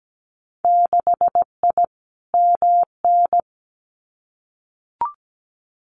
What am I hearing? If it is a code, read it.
Morse code, heard as a keyed tone, about 17 wpm, 700 Hz. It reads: 6I MN
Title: Winter Field Day Exchanges Encore 17wpm